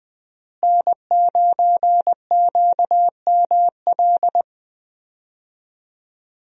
Morse code, heard as a keyed tone, about 21 wpm, 700 Hz. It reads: N9QML
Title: USA Call Signs - 5 Characters - Repeated 2x - 20wpm